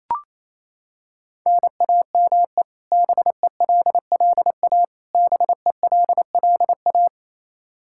Morse code, heard as a keyed tone, about 28 wpm, 700 Hz. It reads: NAME BELLA BELLA
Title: Practice QSO - Exchange 1 - Beginning - 28wpm